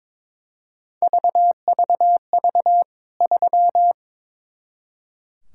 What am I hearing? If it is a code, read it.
Morse code, heard as a keyed tone, about 22 wpm, 700 Hz. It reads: VVV 3
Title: K1USN SST QSO Practice 22wpm